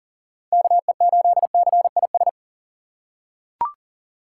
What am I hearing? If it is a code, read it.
Morse code, heard as a keyed tone, about 40 wpm, 700 Hz. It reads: KE8CIS
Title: USA Call Signs - 6 Characters - 4x Character Spacing - 40wpm